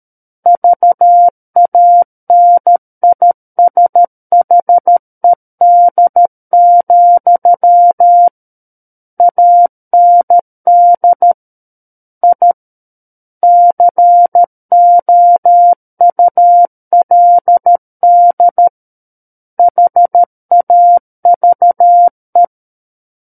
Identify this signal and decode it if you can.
Morse code, heard as a keyed tone, about 13 wpm, 700 Hz. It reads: VANISHED, AND I COULD HAVE
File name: Book One / 04-Chapter 4 - 13 WPM